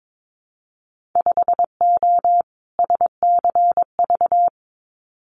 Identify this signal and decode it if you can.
Morse code, heard as a keyed tone, about 22 wpm, 700 Hz. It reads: 5O SCV
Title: Winter Field Day Exchanges Encore 22wpm